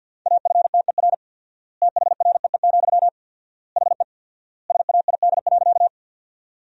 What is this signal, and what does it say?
Morse code, heard as a keyed tone, about 50 wpm, 700 Hz. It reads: AFTER THREE, HE SAID.
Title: 4x Word-Spacing - Sentences from Top 200 Words - 50wpm